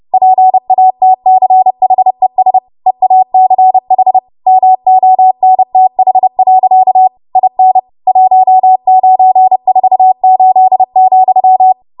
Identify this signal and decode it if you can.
Morse code, heard as a keyed tone, about 30 wpm, 750 Hz. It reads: PATCHES EACH MONTH. IN 1948,
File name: Archive / 30 WPM / 30WPM